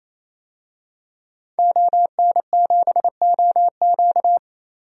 Morse code, heard as a keyed tone, about 28 wpm, 700 Hz. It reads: ON7OQ
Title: DXCC-100 Call Signs - 5 Characters - 28wpm